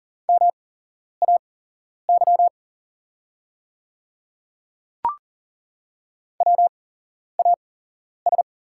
Morse code, heard as a keyed tone, about 40 wpm, spaced as for 11 wpm, 690 Hz. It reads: MAY WAS
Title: 8x Character Spacing - Top 200 Words - 40wpm